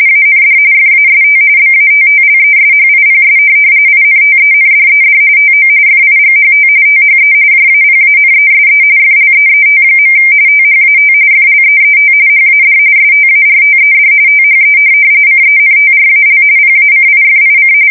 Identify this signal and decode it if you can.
RTTY, about 45.45 baud. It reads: AND FAST BATTLESHIPS INTENDED TO FIGHT THE JAPANESE FLEET. THAT FLEET HAD STARTED THE CENTRAL PACIFIC
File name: fsk170 / USS-New-Jersey